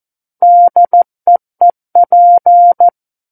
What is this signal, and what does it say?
Morse code, heard as a keyed tone, about 14 wpm, 700 Hz. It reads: DEEP
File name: Book One / 05-Chapter 5 - 14 WPM